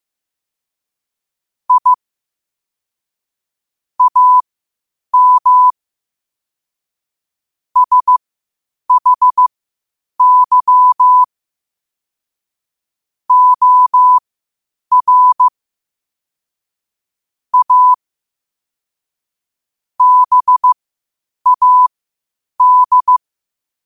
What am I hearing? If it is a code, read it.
Morse code, heard as a keyed tone, about 15 wpm, spaced as for 8 wpm, 1000 Hz. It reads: I AM SHY OR A BAD
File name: qotdfiles / 8wpm